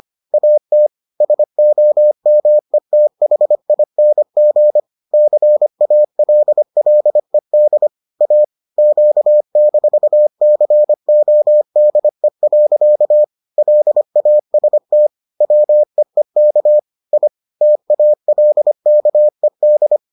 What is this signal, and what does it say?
Morse code, heard as a keyed tone, about 25 wpm, 600 Hz. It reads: AT SOMETHING CALLED A Q-CODE. LAST WEEK I TALKED